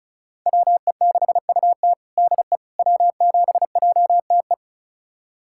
Morse code, heard as a keyed tone, about 36 wpm, 700 Hz. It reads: WE6UT DE W7JTE
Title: USA Callsign DE Callsign - 5 Characters - 35wpm